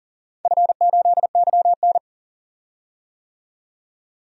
Morse code, heard as a keyed tone, about 40 wpm, 700 Hz. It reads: F8YN
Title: DXCC-100 Call Signs - 4 Characters - 40wpm